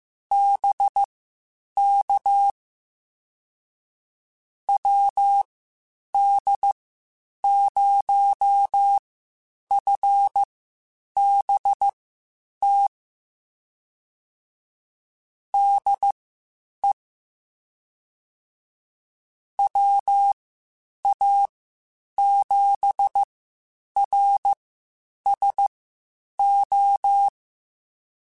Morse code, heard as a keyed tone, about 15 wpm, spaced as for 8 wpm, 800 Hz. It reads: BK WD0FBT DE WA7RSO
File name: QSOs / QSOs-mp3s / WD0FBT 1@13